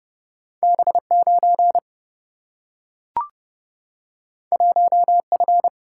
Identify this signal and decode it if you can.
Morse code, heard as a keyed tone, about 30 wpm, 700 Hz. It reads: B 9 1 F
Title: Single-Letter Single-Number 30wpm